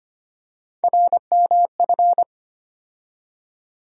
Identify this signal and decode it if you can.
Morse code, heard as a keyed tone, about 25 wpm, 700 Hz. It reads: RMF
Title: USA Call Sign Postfixes - 3 Letters 25wpm